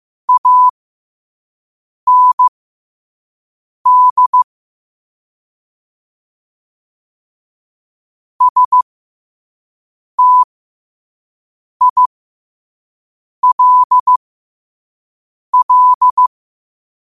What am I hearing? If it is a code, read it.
Morse code, heard as a keyed tone, about 15 wpm, spaced as for 5 wpm, 1000 Hz. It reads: AND STILL
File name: qotdfiles / 5wpm